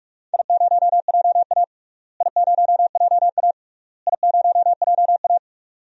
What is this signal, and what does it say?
Morse code, heard as a keyed tone, about 41 wpm, 700 Hz. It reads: I0JA I0JA I0JA
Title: DXCC-100 Call Signs - 4 Characters - Repeated 3x - 45wpm